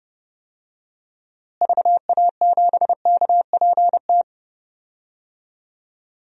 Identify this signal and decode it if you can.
Morse code, heard as a keyed tone, about 32 wpm, 700 Hz. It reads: VA7KPT
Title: DXCC-100 Call Signs - 6 Characters - Repeated 2x - 30wpm